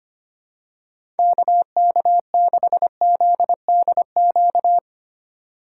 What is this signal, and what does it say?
Morse code, heard as a keyed tone, about 25 wpm, 700 Hz. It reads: KK6ZDQ
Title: USA Call Signs - 6 Characters - 3x Character Spacing - 25wpm